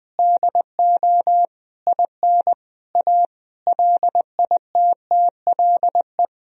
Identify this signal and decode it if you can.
Morse code, heard as a keyed tone, about 20 wpm, 700 Hz. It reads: DO IN A LITTLE
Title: Sentences from Top 1000 Words - Review Part 2 - 20wpm